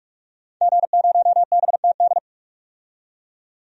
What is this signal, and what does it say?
Morse code, heard as a keyed tone, about 46 wpm, 700 Hz. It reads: G0BTD
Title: DXCC-100 Call Signs - 5 Characters - 45wpm